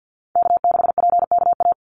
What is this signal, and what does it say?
Morse code, heard as a keyed tone, about 50 wpm, 700 Hz. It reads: K6PKA
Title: USA Call Signs 50wpm